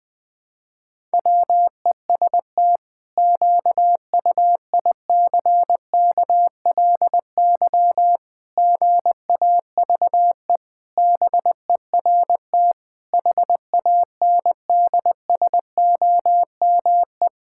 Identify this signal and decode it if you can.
Morse code, heard as a keyed tone, about 20 wpm, 700 Hz. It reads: WEST QUICKLY GAVE BERT HANDSOME
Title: Pangram Challenge v2 - 20wpm